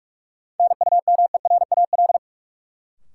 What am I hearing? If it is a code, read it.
Morse code, heard as a keyed tone, about 46 wpm, 700 Hz. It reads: NUMERAL
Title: Sentences from Top 1000 Words - Review Part 2 - New Word Drill - 45wpm